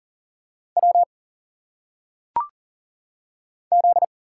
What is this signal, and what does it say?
Morse code, heard as a keyed tone, about 40 wpm, 700 Hz. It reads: W Z
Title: Single Letters - Phonetic Pronunciation 40wpm